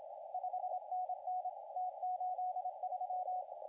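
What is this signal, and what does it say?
Morse code, heard as a keyed tone, about 26 wpm, 730 Hz, with radio noise. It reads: 5NN T83
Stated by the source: CW contest audio recorded at LA4C during the SAC CW contest.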